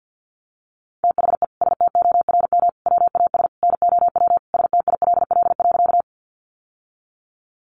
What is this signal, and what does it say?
Morse code, heard as a keyed tone, about 50 wpm, 700 Hz. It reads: THE STORM WAS NOW STILL.
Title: Sentences from Top 1500 Words 50wpm